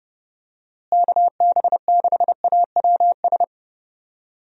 Morse code, heard as a keyed tone, about 30 wpm, 700 Hz. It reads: KB6AWS
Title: USA Call Signs - 6 Characters - Repeated 3x - 30wpm